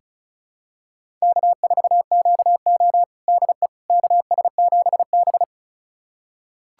Morse code, heard as a keyed tone, about 36 wpm, 700 Hz. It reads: K4QO DE KS7B
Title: USA Callsign DE Callsign - 4 Characters - 35wpm